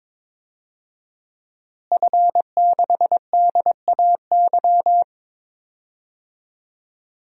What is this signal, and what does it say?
Morse code, heard as a keyed tone, about 21 wpm, 700 Hz. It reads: F6DAY
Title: DXCC-100 Call Signs - 5 Characters - Repeated 3x - 22wpm